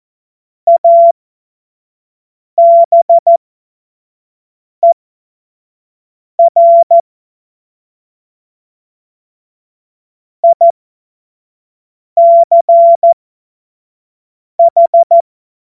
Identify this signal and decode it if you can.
Morse code, heard as a keyed tone, about 14 wpm, spaced as for 5 wpm, 680 Hz. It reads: ABER ICH
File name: Morsekurs / Witze / Witz 03. Mai 2022 25 F70